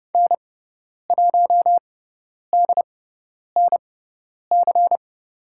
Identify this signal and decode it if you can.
Morse code, heard as a keyed tone, about 30 wpm, spaced as for 10 wpm, 700 Hz. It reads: N1DNC
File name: Lesson 16g - Letter c - Call Signs 30wpm